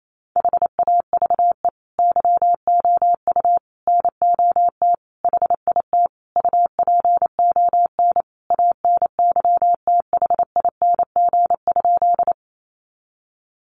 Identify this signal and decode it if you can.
Morse code, heard as a keyed tone, about 28 wpm, 700 Hz. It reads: HAVE YOU NOT HIT UPON ANYTHING?
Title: Sentences from Top 1200 Words 28wpm